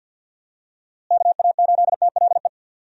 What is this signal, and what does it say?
Morse code, heard as a keyed tone, about 46 wpm, 700 Hz. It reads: KA8TLE
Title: USA Call Signs - 6 Characters - Repeated 2x - 50wpm